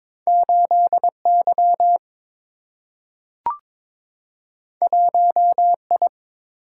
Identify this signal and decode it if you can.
Morse code, heard as a keyed tone, about 23 wpm, 700 Hz. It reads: 8 Y 1 I
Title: Single-Letter Single-Number 22wpm